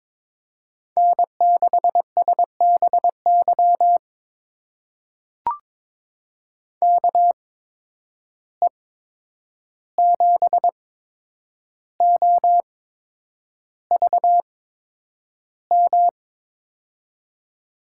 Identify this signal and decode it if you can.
Morse code, heard as a keyed tone, about 22 wpm, spaced as for 9 wpm, 700 Hz. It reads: N6SBY KE7OVM
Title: USA Call Signs - 8x Character Spacing - 22wpm